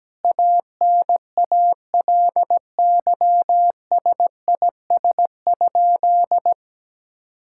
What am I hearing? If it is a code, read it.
Morse code, heard as a keyed tone, about 17 wpm, 700 Hz. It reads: ANALYSIS?
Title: Sentences from Top 1000 Words - Review Part 5 - 17wpm